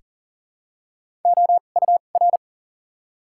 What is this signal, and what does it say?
Morse code, heard as a keyed tone, about 40 wpm, spaced as for 29 wpm, 700 Hz. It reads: OUR
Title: 2x Character Spacing - Top 200 Words - 40wpm